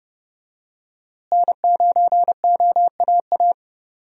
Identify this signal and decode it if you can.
Morse code, heard as a keyed tone, about 30 wpm, 700 Hz. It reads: N9OAA
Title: USA Call Signs - 5 Characters - Repeated 2x - 30wpm